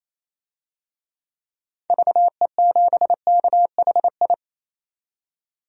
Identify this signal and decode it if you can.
Morse code, heard as a keyed tone, about 28 wpm, 700 Hz. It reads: VE7KHI
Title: DXCC-100 Call Signs - 6 Characters - 28wpm